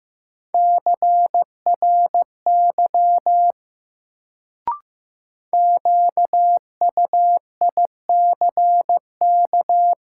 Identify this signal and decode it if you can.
Morse code, heard as a keyed tone, about 15 wpm, 700 Hz. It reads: CRY QUICK
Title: Sentences from Top 500 Words - New Word Drill - 15wpm